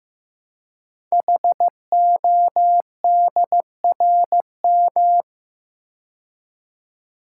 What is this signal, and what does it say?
Morse code, heard as a keyed tone, about 15 wpm, 700 Hz. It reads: HODRM
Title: Code Groups - 5 Letters - 15wpm